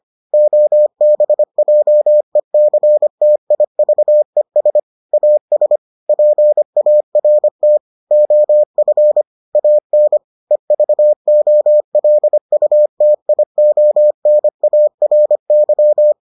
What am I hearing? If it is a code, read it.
Morse code, heard as a keyed tone, about 25 wpm, 600 Hz. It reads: OBJECTIVES AS PART OF AN EVOLUTIONARY